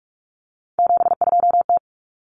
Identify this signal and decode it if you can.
Morse code, heard as a keyed tone, about 44 wpm, 700 Hz. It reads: 72T
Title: Common Cut Numbers - 3-Digit 45wpm